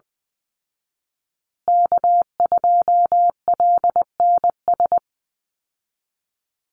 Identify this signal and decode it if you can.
Morse code, heard as a keyed tone, about 20 wpm, 700 Hz. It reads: K2LNS
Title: Call Signs Repeated 3x - 20wpm